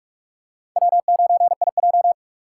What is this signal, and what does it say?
Morse code, heard as a keyed tone, about 46 wpm, 700 Hz. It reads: W9IJ
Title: USA Call Signs - 4 Characters - 2x Character Spacing - 45wpm